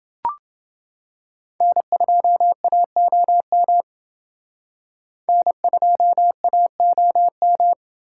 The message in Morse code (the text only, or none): N2AOM N2AOM